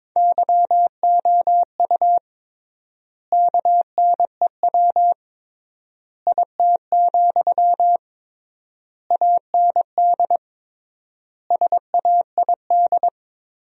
Morse code, heard as a keyed tone, about 22 wpm, 700 Hz. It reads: YOU KNEW IT, AND SAID
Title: 3x Word-Spacing - Sentences from Top 500 Words - 22wpm